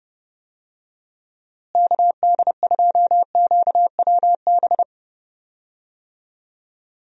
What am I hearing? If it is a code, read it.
Morse code, heard as a keyed tone, about 31 wpm, 700 Hz. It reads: KD2QWB
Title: USA Call Signs - 6 Characters - Repeated 3x - 30wpm